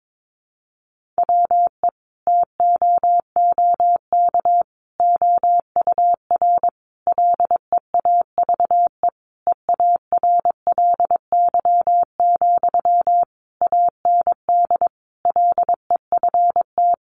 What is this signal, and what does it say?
Morse code, heard as a keyed tone, about 22 wpm, 700 Hz. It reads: WE TOOK OUR LEAVE EARLY, AND LEFT
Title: Sentences from Top 400 Words 22wpm